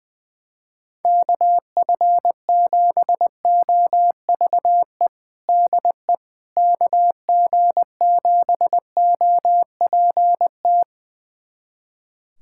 Morse code, heard as a keyed tone, about 20 wpm, 700 Hz. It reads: KF7OVE DE KG7OPT
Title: USA Callsign DE Callsign - 6 Characters - 20wpm